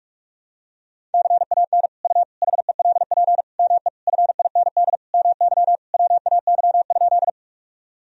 Morse code, heard as a keyed tone, about 47 wpm, 700 Hz. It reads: CAN U HELP ME FIND MY WAY?
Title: Sentences from Top 200 Words and Abbreviations 45wpm